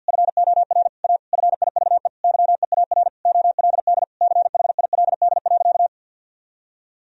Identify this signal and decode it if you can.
Morse code, heard as a keyed tone, about 50 wpm, 700 Hz. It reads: FOR A FIVE YEAR OLD CHILD.
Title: Sentences from Top 1000 Words - Review Part 2 - 50wpm